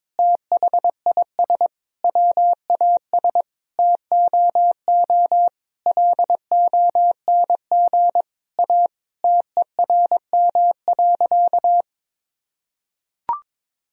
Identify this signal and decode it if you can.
Morse code, heard as a keyed tone, about 22 wpm, 700 Hz. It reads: THIS WAS TOO LONG A TERM.
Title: Sentences from Top 1000 Words - Review Part 4 - 22wpm